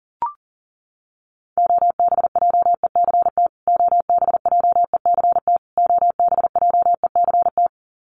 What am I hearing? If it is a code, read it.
Morse code, heard as a keyed tone, about 40 wpm, 700 Hz. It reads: OBJECT OBJECT OBJECT
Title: Top 500 Words Repeated 40wpm